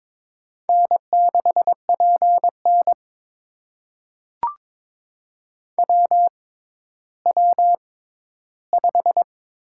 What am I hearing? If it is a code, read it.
Morse code, heard as a keyed tone, about 22 wpm, spaced as for 13 wpm, 700 Hz. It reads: N6PN WW5
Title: USA Call Signs - 4 Characters - 6x Character Spacing - 22wpm